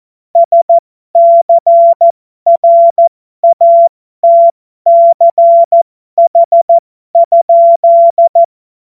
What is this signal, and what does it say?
Morse code, heard as a keyed tone, about 14 wpm, spaced as for 12 wpm, 680 Hz. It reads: SCRATCH?
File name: Morsekurs / Witze